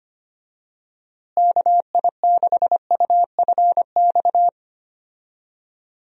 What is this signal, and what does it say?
Morse code, heard as a keyed tone, about 25 wpm, 700 Hz. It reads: KI6UFX
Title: USA Call Signs - 6 Characters - Repeated 2x - 25wpm